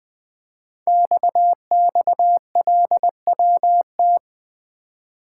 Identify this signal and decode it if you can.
Morse code, heard as a keyed tone, about 20 wpm, 700 Hz. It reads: XXLWT
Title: Code Groups - 5 Letters - 20wpm